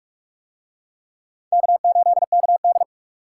Morse code, heard as a keyed tone, about 43 wpm, 700 Hz. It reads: K8KD